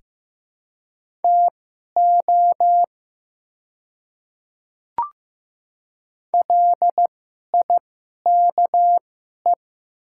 Morse code, heard as a keyed tone, about 15 wpm, spaced as for 11 wpm, 700 Hz. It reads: TO LIKE
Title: 2x Character Spacing - Top 200 Words - 15wpm